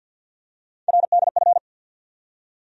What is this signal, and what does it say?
Morse code, heard as a keyed tone, about 53 wpm, 700 Hz. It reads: RDF
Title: USA Call Sign Postfixes - 3 Letters 50wpm